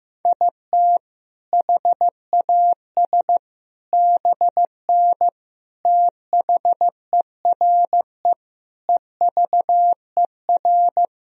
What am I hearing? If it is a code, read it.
Morse code, heard as a keyed tone, about 15 wpm, 700 Hz. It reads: IT HAS BN THERE EVER
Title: Sentences from Top 400 Words and Abbreviations 15wpm